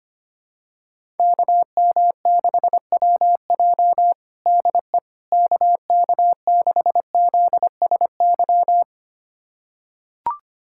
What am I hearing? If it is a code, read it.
Morse code, heard as a keyed tone, about 25 wpm, 700 Hz. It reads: KM6WJ DE KK6ZSY
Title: USA Callsign DE Callsign 25wpm